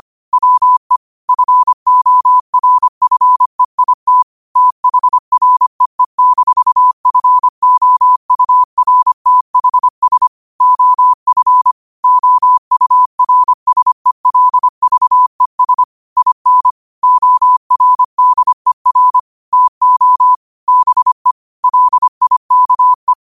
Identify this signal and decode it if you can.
Morse code, heard as a keyed tone, about 25 wpm, 1000 Hz. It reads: WE FORFEIT THREE-FOURTHS OF OURSELVES IN ORDER TO BE LIKE